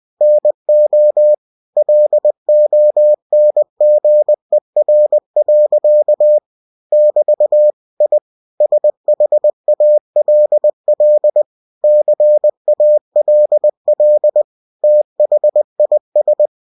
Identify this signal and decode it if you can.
Morse code, heard as a keyed tone, about 20 wpm, 600 Hz. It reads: NO LONGER. <BT> I SHALL CALL THIS
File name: cw-books / winnie-the-pooh / winnie-20wpm-page-0100